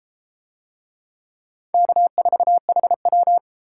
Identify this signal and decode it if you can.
Morse code, heard as a keyed tone, about 32 wpm, 700 Hz. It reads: K4HW